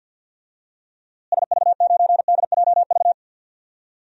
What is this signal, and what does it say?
Morse code, heard as a keyed tone, about 53 wpm, 700 Hz. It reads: SV9DJV
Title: DXCC-100 Call Signs - 6 Characters - Repeated 2x - 50wpm